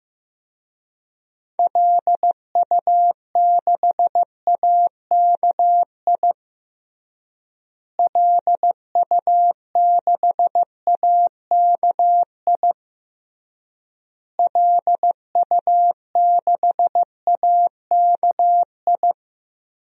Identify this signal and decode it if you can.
Morse code, heard as a keyed tone, about 15 wpm, 700 Hz. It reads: LU6AKI LU6AKI LU6AKI
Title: DXCC-100 Call Signs - 6 Characters - Repeated 3x - 15wpm